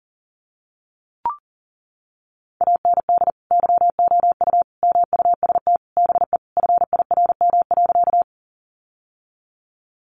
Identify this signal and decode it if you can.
Morse code, heard as a keyed tone, about 40 wpm, 700 Hz. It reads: AND YOU MUST BE FIRM.
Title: Sentences from Top 1400 Words 40wpm